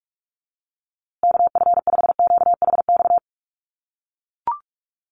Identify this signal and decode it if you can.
Morse code, heard as a keyed tone, about 44 wpm, 700 Hz. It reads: KF5QHX
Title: USA Call Signs Repeated 2x 45wpm